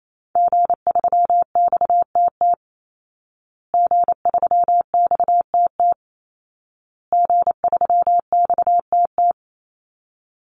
Morse code, heard as a keyed tone, about 28 wpm, 700 Hz. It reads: G3XTT G3XTT G3XTT
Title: Call Signs Repeated 2x - 28wpm